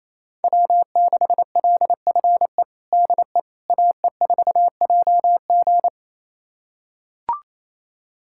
Morse code, heard as a keyed tone, about 28 wpm, 700 Hz. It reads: W6LFE DE AE4JG
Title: USA Callsign DE Callsign - 5 Characters - 28wpm